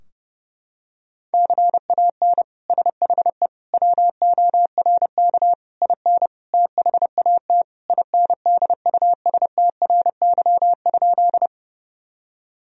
Morse code, heard as a keyed tone, about 32 wpm, 700 Hz. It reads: CAN SHE WORK IN THAT INDUSTRY?
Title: Sentences from Top 1000 Words - Review Part 4 - 30wpm